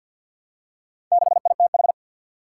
Morse code, heard as a keyed tone, about 47 wpm, 700 Hz. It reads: 6ITH
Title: Four Challenging Letter-Number Combinations 50wpm